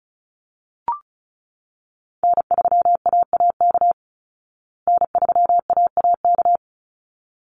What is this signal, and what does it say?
Morse code, heard as a keyed tone, about 34 wpm, 700 Hz. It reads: N3AAK N3AAK
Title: Call Signs Repeated 3x - 35wpm